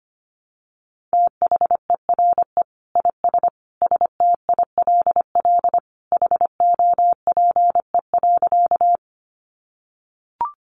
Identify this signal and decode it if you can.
Morse code, heard as a keyed tone, about 25 wpm, 700 Hz. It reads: THERE IS STILL HOPE.
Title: Sentences from Top 1000 Words 25wpm